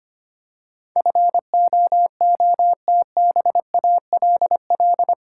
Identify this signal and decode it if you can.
Morse code, heard as a keyed tone, about 25 wpm, 700 Hz. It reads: FOOTBALL
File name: 8 Letter Words - First 3 Letters Given - 25wpm